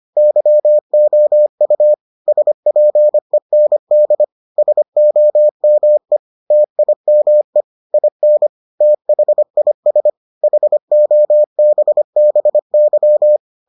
Morse code, heard as a keyed tone, about 25 wpm, 600 Hz. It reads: YOU SPEND SOME TIME IN THIS HOBBY